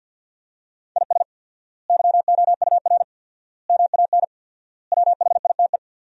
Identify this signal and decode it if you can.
Morse code, heard as a keyed tone, about 49 wpm, 700 Hz. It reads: IS YOUR MAN WHITE
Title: 4x Word-Spacing - Sentences from Top 400 Words - 50wpm